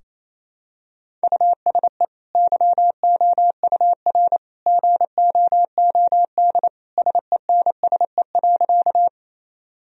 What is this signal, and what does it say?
Morse code, heard as a keyed tone, about 27 wpm, 700 Hz. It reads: USE YOUR GOOD SENSE.
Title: Sentences from Top 1000 Words - Review Part 4 - 28wpm